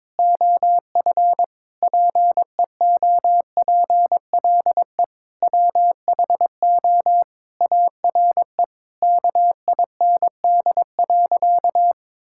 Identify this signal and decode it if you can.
Morse code, heard as a keyed tone, about 22 wpm, 700 Hz. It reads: OF PEOPLE WHO ARE KIND.
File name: practice / 22wpm / Sentences with 8-Word Phrases - 22wpm